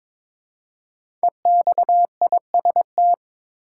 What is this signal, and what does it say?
Morse code, heard as a keyed tone, about 23 wpm, 700 Hz. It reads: EXIST
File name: 5 Letter Words - First 3 Letters Given - 22wpm